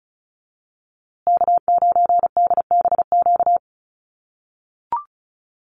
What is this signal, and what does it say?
Morse code, heard as a keyed tone, about 34 wpm, 700 Hz. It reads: K9DBQ
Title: USA Call Signs Repeated 3x 35wpm